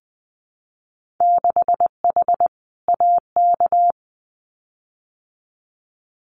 Morse code, heard as a keyed tone, about 20 wpm, 700 Hz. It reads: 6H AK
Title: Winter Field Day Exchanges Encore 20wpm